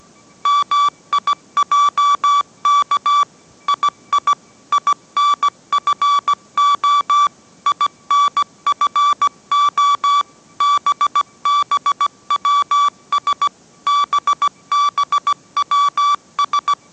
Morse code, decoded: MIJK II INFO INFO BBWS BBWS

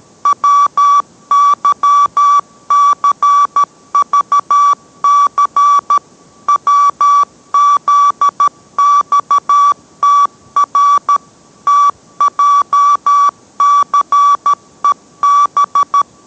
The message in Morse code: WYCVC WZXTR TJCEB

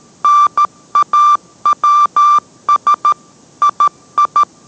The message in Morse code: NAWS II

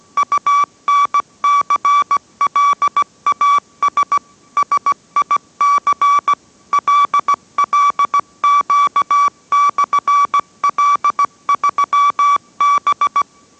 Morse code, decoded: UNCLAS SIC LLQ/L3B

